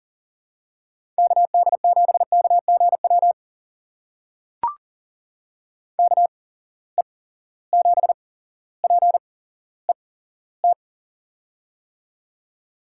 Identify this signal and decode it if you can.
Morse code, heard as a keyed tone, about 40 wpm, spaced as for 17 wpm, 700 Hz. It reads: KD7KGW KE7PET
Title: USA Call Signs - 6 Characters - 8x Character Spacing - 40wpm